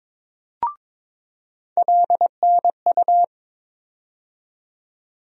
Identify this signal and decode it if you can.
Morse code, heard as a keyed tone, about 22 wpm, 700 Hz. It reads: LNU
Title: USA Call Sign Postfixes - 3 Letters 22wpm